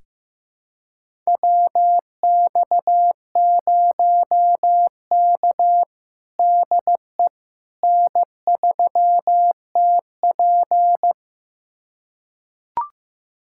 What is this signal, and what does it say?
Morse code, heard as a keyed tone, about 15 wpm, 700 Hz. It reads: WX0K DE N3TP
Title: USA Callsign DE Callsign - 4 Characters - 15wpm